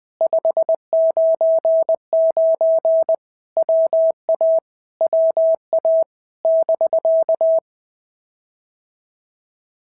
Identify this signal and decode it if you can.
Morse code, heard as a keyed tone, about 20 wpm, 650 Hz. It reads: 599 WA WA <BK>
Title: POTA QSO Practice - 20wpm